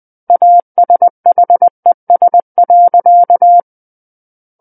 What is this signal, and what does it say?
Morse code, heard as a keyed tone, about 20 wpm, 700 Hz. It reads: ASHES.
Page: SKCC Learning Center - Morse Code Practice Files